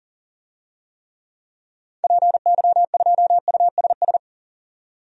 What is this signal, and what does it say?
Morse code, heard as a keyed tone, about 40 wpm, 700 Hz. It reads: PY2USS
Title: DXCC-100 Call Signs - 6 Characters - 40wpm